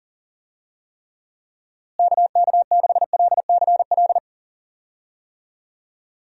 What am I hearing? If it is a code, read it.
Morse code, heard as a keyed tone, about 40 wpm, 700 Hz. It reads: KK6LCL
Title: USA Call Signs - 6 Characters - Repeated 2x - 40wpm